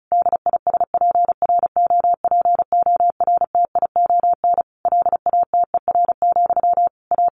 Morse code, heard as a keyed tone, about 34 wpm, 700 Hz. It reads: DISPROPORTION LATER, A
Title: A Sound Of Thunder 35wpm